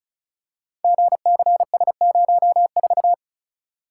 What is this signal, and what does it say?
Morse code, heard as a keyed tone, about 35 wpm, 700 Hz. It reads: GCS04
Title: Code Groups - 5 Letter:Numbers - 35wpm